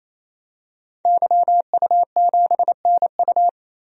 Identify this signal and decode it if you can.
Morse code, heard as a keyed tone, about 28 wpm, 700 Hz. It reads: YU7NU
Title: DXCC-100 Call Signs - 5 Characters - 28wpm